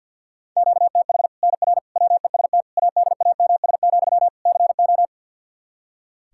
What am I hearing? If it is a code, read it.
Morse code, heard as a keyed tone, about 50 wpm, 700 Hz. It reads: QTH NR WEST ADAMS, CO
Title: Practice QSO - Exchange 1 - Beginning - 50wpm